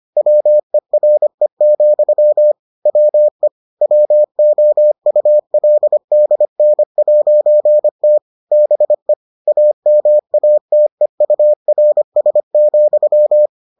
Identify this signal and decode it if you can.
Morse code, heard as a keyed tone, about 25 wpm, 600 Hz. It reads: WERE, WE WOULDN'T BE AMATEURS,